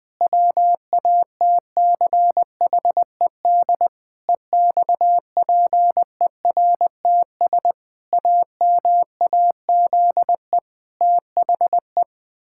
Morse code, heard as a keyed tone, about 20 wpm, 700 Hz. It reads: WATCHED EXPERTS AMAZE THE
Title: Pangram Challenge v2 - 20wpm